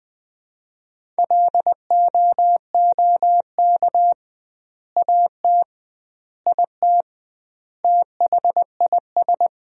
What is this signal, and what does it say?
Morse code, heard as a keyed tone, about 20 wpm, 700 Hz. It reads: LOOK AT IT THIS
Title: 2x Word-Spacing - Sentences from Top 100 Words - 20wpm